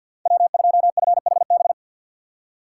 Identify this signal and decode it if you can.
Morse code, heard as a keyed tone, about 52 wpm, 700 Hz. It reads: W2FHB
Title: USA Call Signs - 5 Characters - 6x Character Spacing - 50wpm